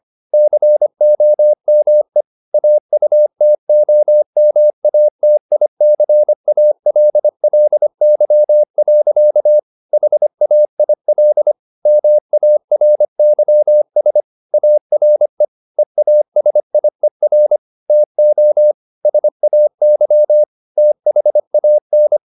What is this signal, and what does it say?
Morse code, heard as a keyed tone, about 25 wpm, 600 Hz. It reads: COME AUTOMATICALLY. HAIL MARYS ARE EASIER TO SAY THAN
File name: the old man and the sea part 1 25wpm